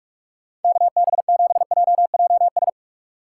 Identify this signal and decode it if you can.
Morse code, heard as a keyed tone, about 57 wpm, 700 Hz. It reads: KB7JJS